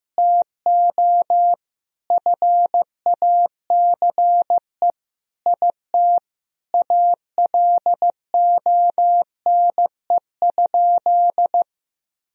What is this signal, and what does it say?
Morse code, heard as a keyed tone, about 15 wpm, 700 Hz. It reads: TO FACE IT ALONE?
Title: Sentences from Top 1000 Words - Review Part 3 - 15wpm